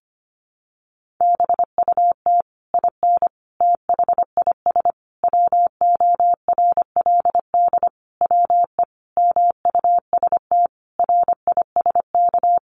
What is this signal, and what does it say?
Morse code, heard as a keyed tone, about 25 wpm, 700 Hz. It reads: BUT IN THIS WORLD WE MUST RISK
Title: Sentences from Top 600 Words 25wpm